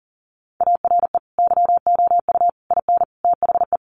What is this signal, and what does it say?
Morse code, heard as a keyed tone, about 40 wpm, 700 Hz. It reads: ARE YOU IN THE
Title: Sentences from Top 1200 Words 40wpm